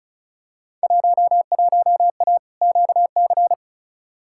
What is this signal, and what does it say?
Morse code, heard as a keyed tone, about 36 wpm, 700 Hz. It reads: 11A QC
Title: Field Day Exchanges Encore - 35wpm